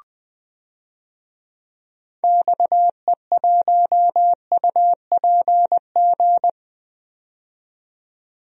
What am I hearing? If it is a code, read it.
Morse code, heard as a keyed tone, about 20 wpm, 700 Hz. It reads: XE1UPG
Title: DXCC-100 Call Signs - 6 Characters - 20wpm